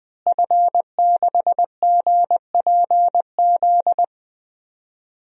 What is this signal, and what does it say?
Morse code, heard as a keyed tone, about 20 wpm, 700 Hz. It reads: F6GPZ
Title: DXCC-100 Call Signs - 5 Characters - Repeated 3x - 20wpm